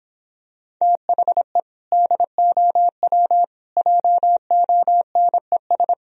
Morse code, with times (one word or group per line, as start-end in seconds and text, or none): 0.8-1.6 THE
1.9-3.4 DOW
3.8-5.9 JONES